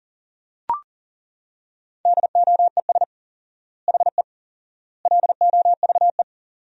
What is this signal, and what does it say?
Morse code, heard as a keyed tone, about 40 wpm, 700 Hz. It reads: DOES HE LOVE
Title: 4x Word-Spacing - Sentences from Top 400 Words - 40wpm